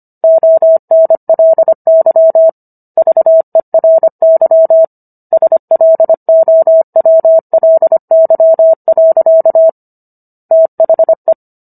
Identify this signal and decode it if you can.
Morse code, heard as a keyed tone, about 25 wpm, 650 Hz. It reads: ONLY VERY SLOWLY. THE